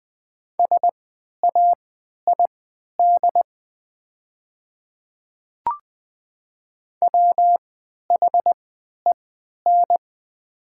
Morse code, heard as a keyed tone, about 20 wpm, spaced as for 11 wpm, 700 Hz. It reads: SAID WHEN
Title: 3x Character Spacing - Top 100 Words - 20wpm